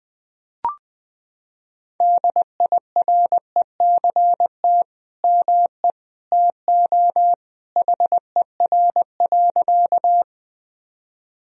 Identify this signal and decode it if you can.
Morse code, heard as a keyed tone, about 20 wpm, 700 Hz. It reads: DIRECT ME TO HER.
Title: Sentences from Top 1000 Words - Review Part 2 - 20wpm